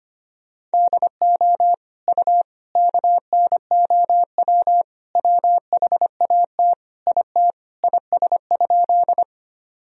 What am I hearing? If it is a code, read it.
Morse code, heard as a keyed tone, about 25 wpm, 700 Hz. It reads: DO U KNOW WHAT IT IS?
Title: Sentences from Top 100 Words and Abbreviations 25wpm